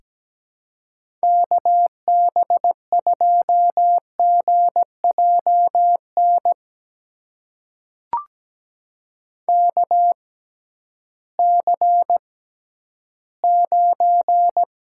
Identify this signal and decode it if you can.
Morse code, heard as a keyed tone, about 17 wpm, spaced as for 11 wpm, 700 Hz. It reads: KB2GJN KC9
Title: USA Call Signs - 6 Characters - 6x Character Spacing - 17wpm